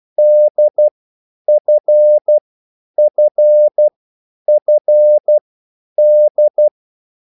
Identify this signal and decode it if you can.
Morse code, heard as a keyed tone, about 12 wpm, spaced as for 9 wpm, 600 Hz. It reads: DFFFD
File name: cwbook